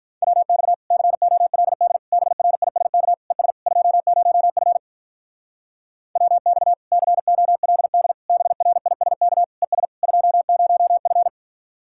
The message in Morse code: WX COLD BRISK ES 20F WX COLD BRISK ES 20F

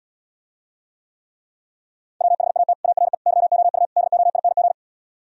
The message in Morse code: WHAT ARE YOU DOING

